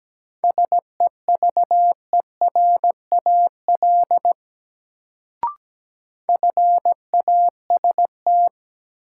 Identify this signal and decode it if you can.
Morse code, heard as a keyed tone, about 17 wpm, 700 Hz. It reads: SEVERAL FAST
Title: Sentences from Top 400 Words - New Word Drill - 17wpm